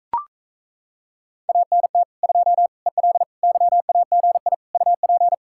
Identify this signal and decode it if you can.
Morse code, heard as a keyed tone, about 37 wpm, 700 Hz. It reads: ANT 2 EL YAGI UP